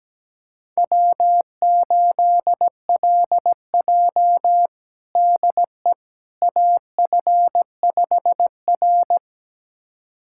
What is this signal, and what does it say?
Morse code, heard as a keyed tone, about 17 wpm, 700 Hz. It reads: W8LJ DE AF5R
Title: USA Callsign DE Callsign - 4 Characters - 17wpm